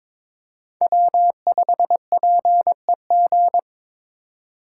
Morse code, heard as a keyed tone, about 22 wpm, 700 Hz. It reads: W5PEG
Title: USA Call Signs - 3x Character Spacing - 22wpm